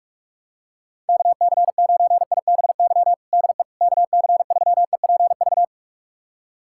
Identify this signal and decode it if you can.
Morse code, heard as a keyed tone, about 42 wpm, 700 Hz. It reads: KC9IBY DE KC3EPV
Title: USA Callsign DE Callsign - 6 Characters - 45wpm